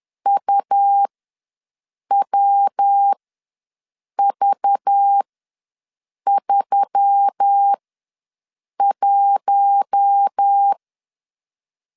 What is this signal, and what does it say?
Morse code, heard as a keyed tone, about 10 wpm, spaced as for 6 wpm, 800 Hz. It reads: UWV31
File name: LEZIONI / 6 PKX0 / 6 PK ws 10wpm 3745